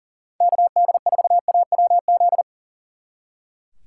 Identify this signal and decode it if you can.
Morse code, heard as a keyed tone, about 49 wpm, 700 Hz. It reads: KD4AWZ
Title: Speed Racing - USA 2x3 Call Signs 40wpm